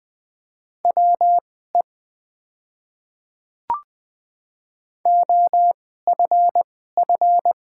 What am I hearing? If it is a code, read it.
Morse code, heard as a keyed tone, about 20 wpm, spaced as for 14 wpm, 700 Hz. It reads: WE OFF
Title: 2x Character Spacing - Top 300 Words - 20wpm